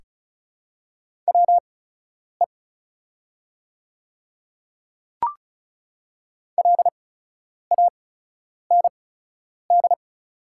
Morse code, heard as a keyed tone, about 36 wpm, spaced as for 10 wpm, 690 Hz. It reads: WE LAND
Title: 8x Character Spacing - Top 200 Words - 35wpm